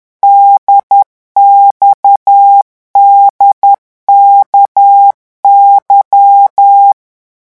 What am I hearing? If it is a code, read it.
Morse code, heard as a keyed tone, about 10 wpm, 800 Hz. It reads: DXDKY
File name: morse mp3 / revision3-vitesse 10 mots